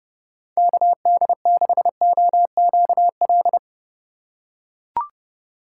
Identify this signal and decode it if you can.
Morse code, heard as a keyed tone, about 30 wpm, 700 Hz. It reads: KD6OQL
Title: USA Call Signs - 2x Character Spacing - 30wpm